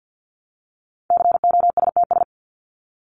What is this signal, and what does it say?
Morse code, heard as a keyed tone, about 50 wpm, 700 Hz. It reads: COSTS
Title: ES:S Suffix - Top 100 Words with Base Words 50wpm